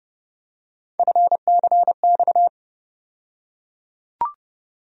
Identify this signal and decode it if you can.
Morse code, heard as a keyed tone, about 30 wpm, 700 Hz. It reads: FCX
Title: USA Call Sign Postfixes - 3 Letters 30wpm